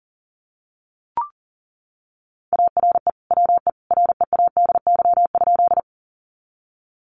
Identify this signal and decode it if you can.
Morse code, heard as a keyed tone, about 40 wpm, 700 Hz. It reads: ARE WE READY?
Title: Sentences from Top 400 Words 40wpm